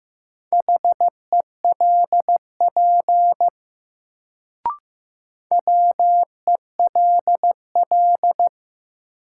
Morse code, heard as a keyed tone, about 15 wpm, 700 Hz. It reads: HELP WELL
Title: Sentences from Top 200 Words - New Word Drill - 15wpm